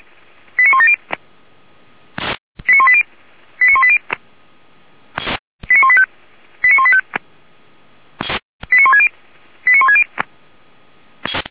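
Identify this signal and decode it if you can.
Five-tone selective call (ZVEI), heard as quick runs of five tones, tones of about 70 ms. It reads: 89170 89180 89180 89185 89185 89160 89160
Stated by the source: ZVEI five-tone selective call received by SDR, Feuerwehr Orlamünde audio server.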